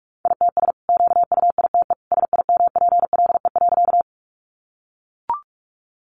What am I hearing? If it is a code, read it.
Morse code, heard as a keyed tone, about 44 wpm, 700 Hz. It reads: ITS QUITE SIMPLE.
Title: Sentences from Top 700 Words 45wpm